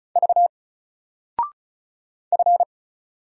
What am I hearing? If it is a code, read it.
Morse code, heard as a keyed tone, about 36 wpm, 700 Hz. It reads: V F
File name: Single-Letter-Number-35wpm